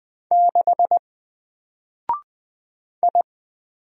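Morse code, heard as a keyed tone, about 20 wpm, 700 Hz. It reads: 6 I
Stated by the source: Single letter and number Morse drill at 20 wpm, from N4LCD.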